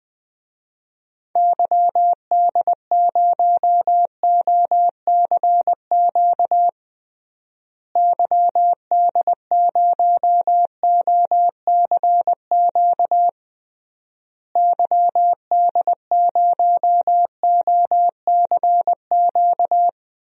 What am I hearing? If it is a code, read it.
Morse code, heard as a keyed tone, about 20 wpm, 700 Hz. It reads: YD0OCQ YD0OCQ YD0OCQ
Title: DXCC-100 Call Signs - 6 Characters - Repeated 3x - 20wpm